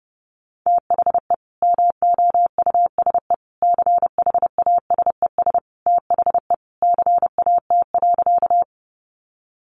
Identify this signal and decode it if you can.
Morse code, heard as a keyed tone, about 30 wpm, 700 Hz. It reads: THE MOUSE CHASES THE CAT.
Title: Easy Phrases 30wpm